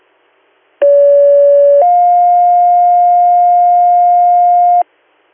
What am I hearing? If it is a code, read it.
Two-tone page, heard as a one-second tone, then a three-second tone, about 570 then 710 Hz, with noise. It is heard from 0.8 to 4.8 s.